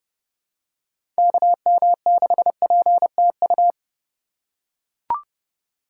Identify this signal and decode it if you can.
Morse code, heard as a keyed tone, about 30 wpm, 700 Hz. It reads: KM6PTU
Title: USA Call Signs - 6 Characters - 2x Character Spacing - 30wpm